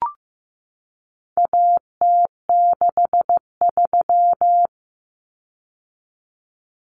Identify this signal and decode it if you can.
Morse code, heard as a keyed tone, about 15 wpm, 700 Hz. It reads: AT63
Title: Common Cut Numbers - 4-Digit 15wpm